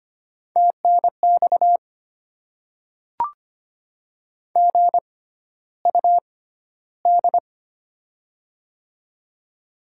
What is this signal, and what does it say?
Morse code, heard as a keyed tone, about 25 wpm, spaced as for 13 wpm, 690 Hz. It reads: TNX GUD
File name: Top 20 QSO Elements - 6x Character Spacing - 25wpm